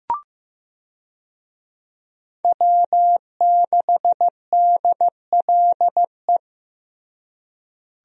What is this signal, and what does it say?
Morse code, heard as a keyed tone, about 15 wpm, 700 Hz. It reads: W6DLE
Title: USA Call Signs - 5 Characters - Repeated 3x - 15wpm